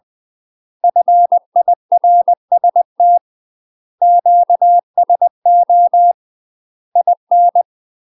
Morse code, heard as a keyed tone, about 20 wpm, 700 Hz. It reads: FIRST QSO IN
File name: UK MASTER QSOs TRAINING / 20WPM / UK MASTER QSOs 0043